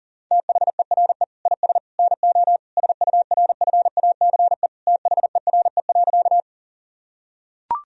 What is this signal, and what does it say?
Morse code, heard as a keyed tone, about 40 wpm, 700 Hz. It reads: THERE IS NO SURFACE THERE.
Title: Sentences from Top 1000 Words - Review Part 3 - 40wpm